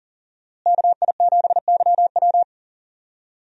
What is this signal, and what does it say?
Morse code, heard as a keyed tone, about 40 wpm, 700 Hz. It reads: KI7YW
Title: USA Call Signs - 5 Characters - Repeated 3x - 40wpm